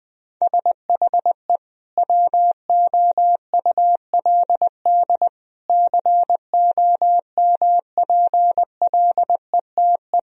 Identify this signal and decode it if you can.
Morse code, heard as a keyed tone, about 21 wpm, 700 Hz. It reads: SHE WOULD COMPLETE
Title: Sentences from Top 1000 Words - Review Part 2 - 20wpm